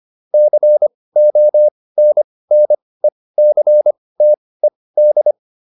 Morse code, heard as a keyed tone, about 25 wpm, spaced as for 18 wpm, 600 Hz. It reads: CONNECTED